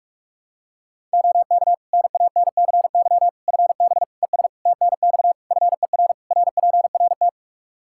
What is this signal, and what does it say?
Morse code, heard as a keyed tone, about 46 wpm, 700 Hz. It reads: OK NANCY FB ES TNX FER RPRT
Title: Practice QSO - Exchange 2 - Info - 45wpm